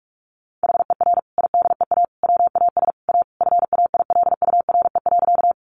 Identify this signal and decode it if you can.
Morse code, heard as a keyed tone, about 44 wpm, 700 Hz. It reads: HER IDEA WAS A FAILURE.
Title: Sentences from Top 900 Words 45wpm